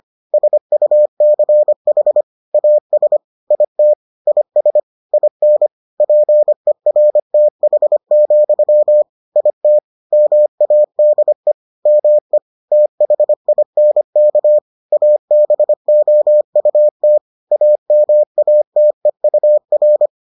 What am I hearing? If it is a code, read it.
Morse code, heard as a keyed tone, about 25 wpm, 600 Hz. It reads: SUCH AS IT IS IN PERTH, IT MADE ME THINK ABOUT AMATEUR